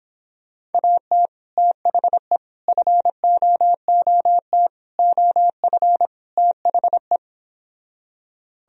Morse code, heard as a keyed tone, about 29 wpm, 700 Hz. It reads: AT THE FOOT OF THE